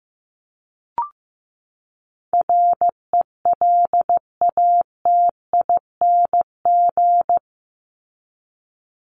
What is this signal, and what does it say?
Morse code, heard as a keyed tone, about 15 wpm, 700 Hz. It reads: RELATING
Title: ING Suffix - Top 100 Words with Base Words 15wpm